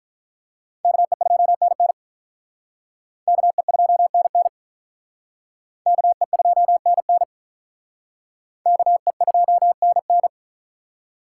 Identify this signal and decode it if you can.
Morse code, heard as a keyed tone, about 44 wpm, 700 Hz. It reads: KE2NN KE2NN KE2NN KE2NN
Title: Speed Racing - USA Call Signs 35wpm